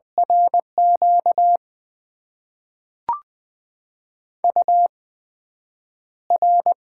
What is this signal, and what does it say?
Morse code, heard as a keyed tone, about 20 wpm, spaced as for 9 wpm, 700 Hz. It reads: RQ UR
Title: Top 90 QSO Elements - 8x Character Spacing - 20wpm